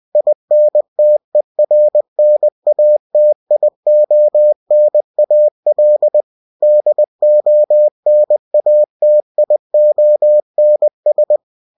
Morse code, heard as a keyed tone, about 20 wpm, 600 Hz. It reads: INTERNATIONAL DONATIONS